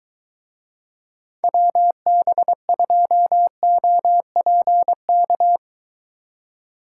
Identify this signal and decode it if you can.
Morse code, heard as a keyed tone, about 23 wpm, 700 Hz. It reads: WB2OPK